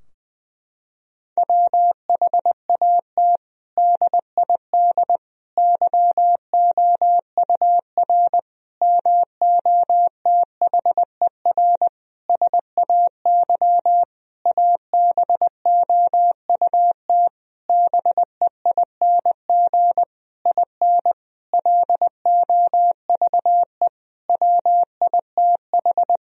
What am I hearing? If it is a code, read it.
Morse code, heard as a keyed tone, about 21 wpm, 700 Hz. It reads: WHAT DID YOUR MOTHER SAY ABOUT BEING IN LOVE WITH
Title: Sentences from Top 1000 Words - Review Part 3 - 20wpm